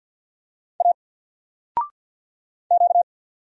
Morse code, heard as a keyed tone, about 48 wpm, 700 Hz. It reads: A Q
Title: Single Letter-Number 50wpm